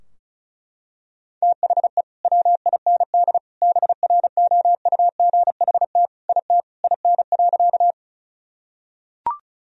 Morse code, heard as a keyed tone, about 34 wpm, 700 Hz. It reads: THE WIND BROUGHT IT IN.
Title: Sentences from Top 1000 Words - Review Part 2 - 35wpm